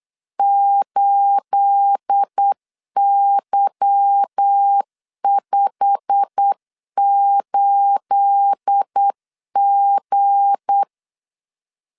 Morse code, heard as a keyed tone, about 8 wpm, 800 Hz. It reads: 8Y58G
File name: LEZIONI / 9 28 8wpm 3512